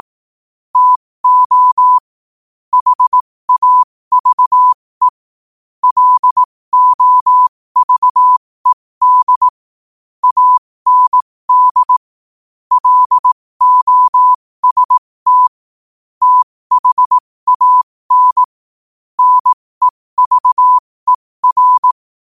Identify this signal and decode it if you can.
Morse code, heard as a keyed tone, about 18 wpm, spaced as for 15 wpm, 1000 Hz. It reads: TO HAVE LOVED AND LOST THAN NEVER
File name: qotdfiles / 15wpm